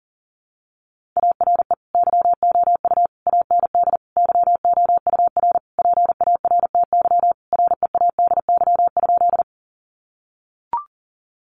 Morse code, heard as a keyed tone, about 40 wpm, 700 Hz. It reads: ARE YOU AND YOUR PARTY READY?
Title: Sentences from Top 1400 Words 40wpm